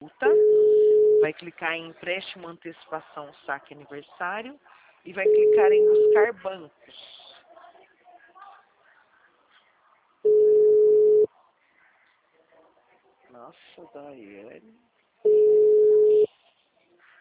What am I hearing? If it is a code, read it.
Ringback tone (Europe, 425 Hz), with speech.